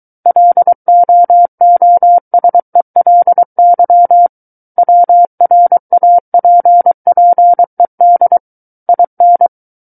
Morse code, heard as a keyed tone, about 23 wpm, 700 Hz. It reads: LOOSELY WRAPPED IN
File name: Book One / 14-Chapter 14 - 23 WPM